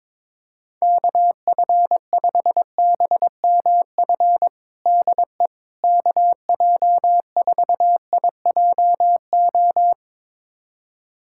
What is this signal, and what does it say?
Morse code, heard as a keyed tone, about 22 wpm, 700 Hz. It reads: KF5BMF DE KJ4IJO
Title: USA Callsign DE Callsign - 6 Characters - 22wpm